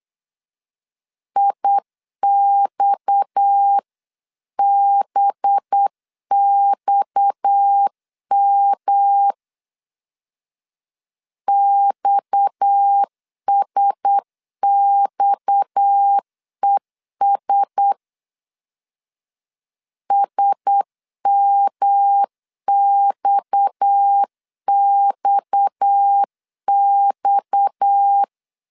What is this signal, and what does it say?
Morse code, heard as a keyed tone, about 8 wpm, 800 Hz. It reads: IXBXM XSXES SMXXX
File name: LEZIONI / 6 PKX0 / 6 PKX 8wpm 3041